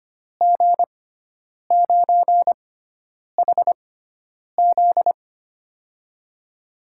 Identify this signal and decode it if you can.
Morse code, heard as a keyed tone, about 25 wpm, spaced as for 9 wpm, 700 Hz. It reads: G9HZ
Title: USA Call Signs - 5 Characters - 6x Character Spacing - 25wpm